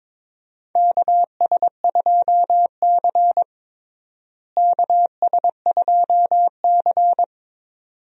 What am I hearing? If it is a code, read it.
Morse code, heard as a keyed tone, about 22 wpm, 700 Hz. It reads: KS2C KS2C
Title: USA Call Signs - 4 Characters - Repeated 2x - 22wpm